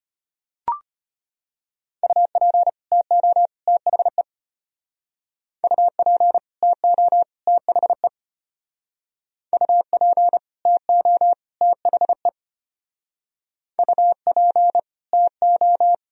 Morse code, heard as a keyed tone, about 33 wpm, 700 Hz. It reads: UP TO THE UP TO THE UP TO THE UP TO